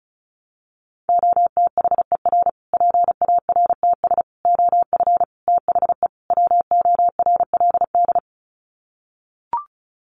Morse code, heard as a keyed tone, about 34 wpm, 700 Hz. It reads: OTHER PARTS OF THE WORLD
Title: Sets of 5 Words Encore 35wpm